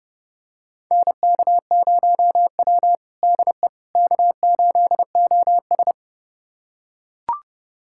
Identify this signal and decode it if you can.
Morse code, heard as a keyed tone, about 30 wpm, 700 Hz. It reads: NK0W DE K8OS
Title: USA Callsign DE Callsign - 4 Characters - 30wpm